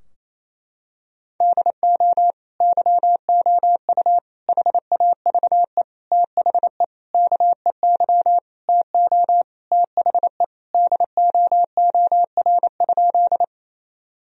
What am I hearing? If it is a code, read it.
Morse code, heard as a keyed tone, about 28 wpm, 700 Hz. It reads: DO YOU HAVE THE KEY TO THE DOOR?
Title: Sentences from Top 1000 Words - Review Part 4 - 28wpm